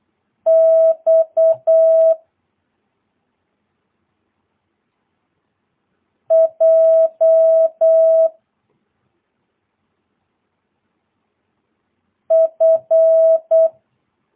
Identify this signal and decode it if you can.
Morse code, heard as a keyed tone, about 8 wpm, 650 Hz. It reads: X J F
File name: jh8cbh / 2022cw